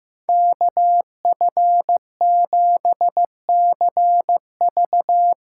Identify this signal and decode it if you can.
Morse code, heard as a keyed tone, about 15 wpm, 700 Hz. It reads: KF7CV
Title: USA Call Signs - 5 Characters - 6x Character Spacing - 15wpm